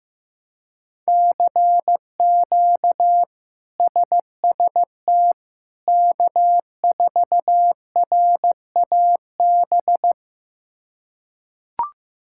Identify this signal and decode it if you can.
Morse code, heard as a keyed tone, about 15 wpm, 700 Hz. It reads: CQ SST K4RAB
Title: K1USN SST QSO Practice - 15wpm